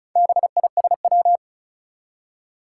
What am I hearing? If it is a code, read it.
Morse code, heard as a keyed tone, about 36 wpm, 690 Hz. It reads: BISW
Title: Four Challenging Letter-Number Combinations 35wpm